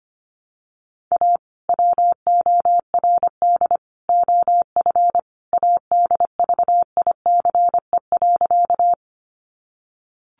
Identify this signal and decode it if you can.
Morse code, heard as a keyed tone, about 25 wpm, 700 Hz. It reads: A WORD OF ADVICE.
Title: Sentences from Top 1100 Words 25wpm